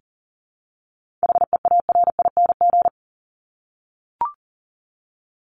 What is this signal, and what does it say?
Morse code, heard as a keyed tone, about 40 wpm, 700 Hz. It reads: HEARING
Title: ING Suffix - Top 100 Words with Base Words 40wpm